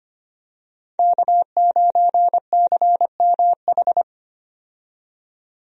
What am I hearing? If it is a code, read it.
Morse code, heard as a keyed tone, about 25 wpm, 700 Hz. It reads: K9CMH
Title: USA Call Signs - 5 Characters - 25wpm